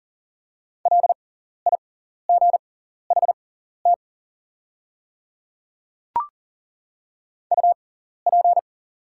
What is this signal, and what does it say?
Morse code, heard as a keyed tone, about 40 wpm, spaced as for 14 wpm, 700 Hz. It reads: LIGHT UP
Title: 6x Character Spacing - Top 200 Words - 40wpm